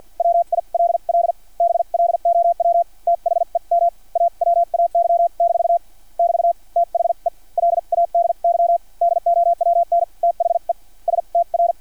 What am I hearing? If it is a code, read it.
Morse code, heard as a keyed tone, about 50 wpm, spaced as for 41 wpm, 690 Hz. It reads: WILL BLOW THEM AWAY- <BT> THE LADY DOWN THE STR
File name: archived / CRSnet